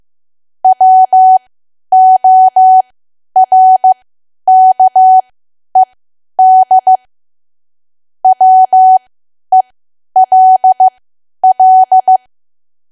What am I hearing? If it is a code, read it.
Morse code, heard as a keyed tone, about 15 wpm, spaced as for 10 wpm, 750 Hz. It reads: WORKED WELL
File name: Archive / 10 WPM / 10WPM